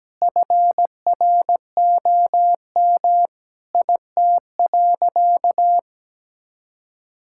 Morse code, heard as a keyed tone, about 17 wpm, 700 Hz. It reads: FROM IT.
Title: Sentences from Top 1000 Words - Review Part 5 - 17wpm